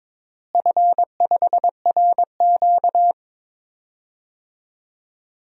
Morse code, heard as a keyed tone, about 22 wpm, 700 Hz. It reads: F5RQ
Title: DXCC-100 Call Signs - 4 Characters - Repeated 3x - 22wpm